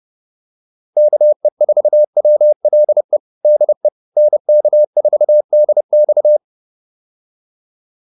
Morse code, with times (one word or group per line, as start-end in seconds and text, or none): 1.0-3.2 KE4WLE
3.4-3.9 DE
4.2-6.4 NK4DX